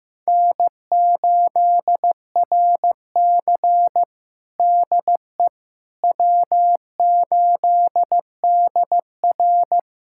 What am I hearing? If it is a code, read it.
Morse code, heard as a keyed tone, about 16 wpm, 700 Hz. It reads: N8RC DE W8DR
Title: USA Callsign DE Callsign - 4 Characters - 15wpm